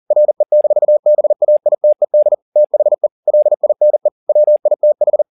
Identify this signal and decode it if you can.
Morse code, heard as a keyed tone, about 40 wpm, 600 Hz. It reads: RE-BAITED THE LINE WITH
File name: the old man and the sea 2 40wpm